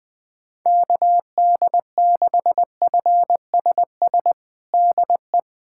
Morse code, heard as a keyed tone, about 20 wpm, 700 Hz. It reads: KD6FSS DE
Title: USA Callsign DE Callsign 20wpm